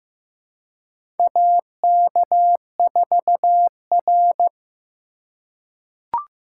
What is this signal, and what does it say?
Morse code, heard as a keyed tone, about 15 wpm, 700 Hz. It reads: AK4R
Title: USA Call Signs - 4 Characters - 4x Character Spacing - 15wpm